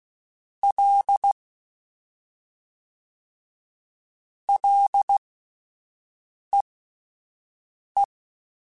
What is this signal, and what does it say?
Morse code, heard as a keyed tone, about 16 wpm, spaced as for 5 wpm, 800 Hz. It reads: L LEE
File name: Ls / Single L m3s / Single L Prefixe 3x2 Twice @5